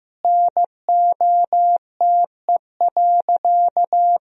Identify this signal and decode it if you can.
Morse code, heard as a keyed tone, about 15 wpm, 700 Hz. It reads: NOTE.
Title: Sentences from Top 1000 Words - Review Part 2 - 15wpm